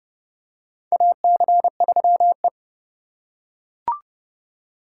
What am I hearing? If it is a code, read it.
Morse code, heard as a keyed tone, about 30 wpm, 700 Hz. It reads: AC3E
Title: USA Call Signs - 4 Characters - 2x Character Spacing - 30wpm